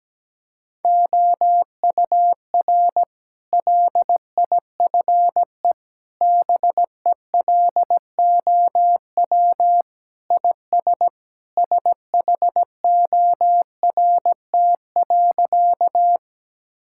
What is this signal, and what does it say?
Morse code, heard as a keyed tone, about 17 wpm, 700 Hz. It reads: OUR LIFE BELOW IS SHORT.
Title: Sentences from Top 1000 Words - Review Part 5 - 17wpm